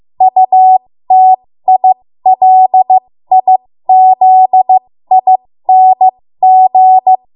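Morse code, heard as a keyed tone, about 15 wpm, spaced as for 13 wpm, 750 Hz. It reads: UTILIZING
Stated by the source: 13 wpm Morse code practice archive transmission.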